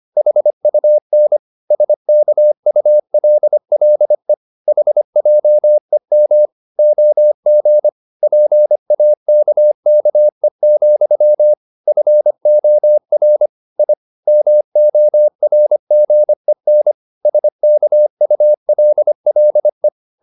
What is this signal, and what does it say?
Morse code, heard as a keyed tone, about 25 wpm, 600 Hz. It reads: HUN SKULLE HJEM OG PAKKE, FOR I MORGEN SKULLE
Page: Ferie 25 wpm